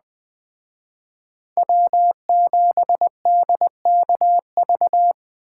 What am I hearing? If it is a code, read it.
Morse code, heard as a keyed tone, about 20 wpm, 700 Hz. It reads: W7DKV